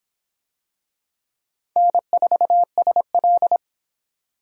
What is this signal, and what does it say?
Morse code, heard as a keyed tone, about 25 wpm, 700 Hz. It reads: N4SL